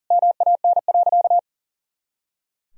Morse code, heard as a keyed tone, about 40 wpm, 700 Hz. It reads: MAN.
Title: Sentences from Top 500 Words and Abbreviations 40wpm